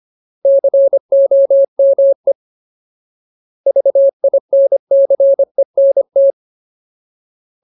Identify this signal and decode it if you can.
Morse code, heard as a keyed tone, about 25 wpm, 550 Hz. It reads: COME VINCENT
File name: Ultimate Word List / 25WPM Ultimate